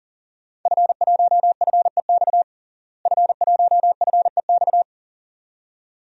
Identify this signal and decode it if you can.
Morse code, heard as a keyed tone, about 37 wpm, 700 Hz. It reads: F1FEX F1FEX
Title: DXCC-100 Call Signs - 5 Characters - Repeated 2x - 40wpm